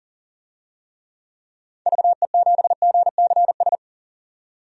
Morse code, heard as a keyed tone, about 40 wpm, 700 Hz. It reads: VE7GCS
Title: DXCC-100 Call Signs - 6 Characters - 40wpm